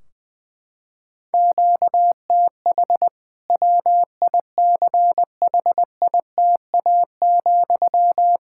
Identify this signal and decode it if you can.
Morse code, heard as a keyed tone, about 20 wpm, 700 Hz. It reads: QTH WICHITA,
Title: Practice QSO - Exchange 1 - Beginning - 20wpm